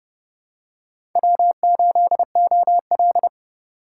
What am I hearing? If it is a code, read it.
Morse code, heard as a keyed tone, about 27 wpm, 700 Hz. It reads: W8OL